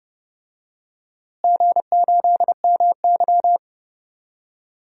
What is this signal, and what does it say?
Morse code, heard as a keyed tone, about 32 wpm, 700 Hz. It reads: G8MY
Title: DXCC-100 Call Signs - 4 Characters - Repeated 3x - 30wpm